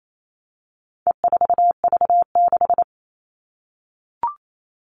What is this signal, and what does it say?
Morse code, heard as a keyed tone, about 28 wpm, 700 Hz. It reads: E4V6
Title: All Cut Numbers - 4-Digit 28wpm